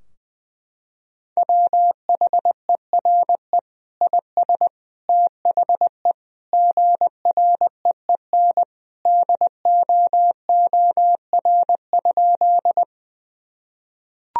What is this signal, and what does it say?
Morse code, heard as a keyed tone, about 20 wpm, 700 Hz. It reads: WHERE IS THE GREEN DOOR?
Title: Sentences from Top 1000 Words - Review Part 2 - 20wpm